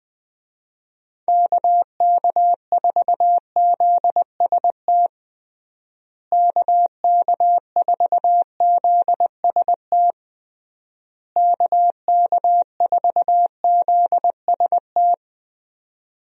Morse code, heard as a keyed tone, about 20 wpm, 700 Hz. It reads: KK4ZST KK4ZST KK4ZST
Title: USA Call Signs - 6 Characters - Repeated 3x - 20wpm